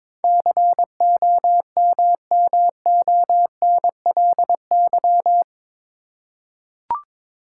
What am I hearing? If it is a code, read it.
Morse code, heard as a keyed tone, about 22 wpm, 700 Hz. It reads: COMMONLY
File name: practice / 22wpm / Eight Letter Words - 22wpm